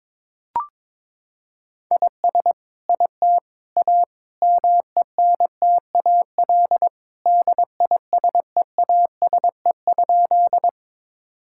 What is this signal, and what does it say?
Morse code, heard as a keyed tone, about 21 wpm, 700 Hz. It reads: IS IT A MENTAL DISEASE?
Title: Sentences from Top 1000 Words - Review Part 6 - 22wpm